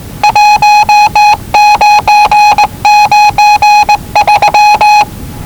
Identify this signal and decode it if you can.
Morse code, heard as a keyed tone, about 19 wpm, 830 Hz. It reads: 1993
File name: WellingtonRadioZlwCloseDownOnHfAndMf / 01-Zlw 4277khz 417.5 qrt until 0925z